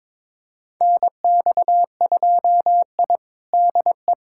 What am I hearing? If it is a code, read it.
Morse code, heard as a keyed tone, about 22 wpm, 700 Hz. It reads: NX2I DE
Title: USA Callsign DE Callsign 22wpm